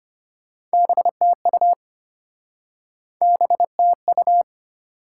Morse code, heard as a keyed tone, about 33 wpm, 700 Hz. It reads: BTU BTU